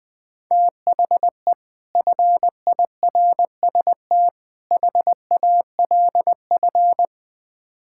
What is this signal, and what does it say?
Morse code, heard as a keyed tone, about 20 wpm, 700 Hz. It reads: THE FIRST HALF
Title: Build A Phrase - 5-Word Phrases - 20wpm